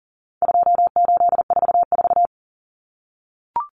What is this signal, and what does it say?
Morse code, heard as a keyed tone, about 40 wpm, 700 Hz. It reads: 2844
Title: Numbers - 4-Digit 40wpm